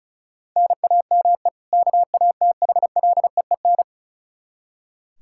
Morse code, heard as a keyed tone, about 36 wpm, 700 Hz. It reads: NAME KATHLEEN
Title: Practice QSO - Exchange 1 - Beginning - 35wpm